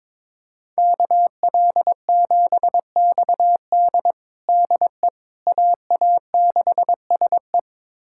Morse code, heard as a keyed tone, about 22 wpm, 700 Hz. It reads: KL7XD DE AA6SE
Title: USA Callsign DE Callsign - 5 Characters - 22wpm